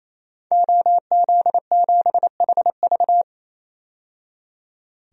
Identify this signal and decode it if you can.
Morse code, heard as a keyed tone, about 28 wpm, 700 Hz. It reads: OZ7HV
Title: DXCC-100 Call Signs - 5 Characters - 28wpm